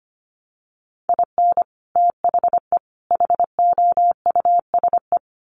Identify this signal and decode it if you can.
Morse code, heard as a keyed tone, about 25 wpm, 700 Hz. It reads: IN THE HOUSE
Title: Sets of 3 Words 25wpm